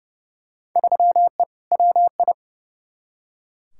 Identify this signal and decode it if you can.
Morse code, heard as a keyed tone, about 30 wpm, 700 Hz. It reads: 3E WI
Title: Field Day Exchanges Encore - 30wpm